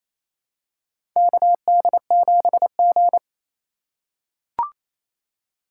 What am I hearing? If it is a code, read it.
Morse code, heard as a keyed tone, about 27 wpm, 700 Hz. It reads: KD7G
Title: USA Call Signs - 4 Characters - 4x Character Spacing - 28wpm